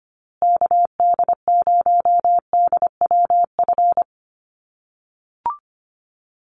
Morse code, heard as a keyed tone, about 25 wpm, 700 Hz. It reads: KD0DWF
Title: USA Call Signs Repeated 2x 25wpm